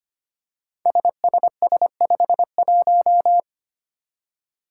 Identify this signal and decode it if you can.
Morse code, heard as a keyed tone, about 25 wpm, 700 Hz. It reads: SSS51
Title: Five Challenging Letter-Number Combinations 25wpm